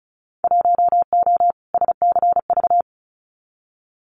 Morse code, heard as a keyed tone, about 34 wpm, 700 Hz. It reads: 1O SCV
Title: Winter Field Day Exchanges 35wpm